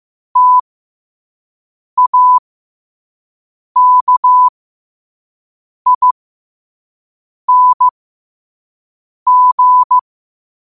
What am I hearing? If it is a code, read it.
Morse code, heard as a keyed tone, about 15 wpm, spaced as for 5 wpm, 1000 Hz. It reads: TAKING
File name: qotdfiles / 5wpm